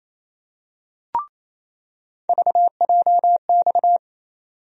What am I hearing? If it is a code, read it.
Morse code, heard as a keyed tone, about 28 wpm, 700 Hz. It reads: VJX
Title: USA Call Sign Postfixes - 3 Letters 28wpm